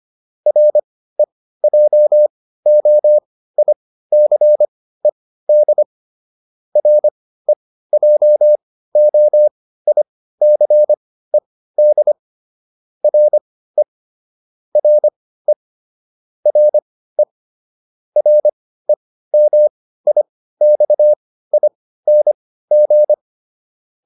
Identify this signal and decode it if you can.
Morse code, heard as a keyed tone, about 25 wpm, spaced as for 15 wpm, 600 Hz. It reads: REJOICED REJOICED RE RE RE REMIXING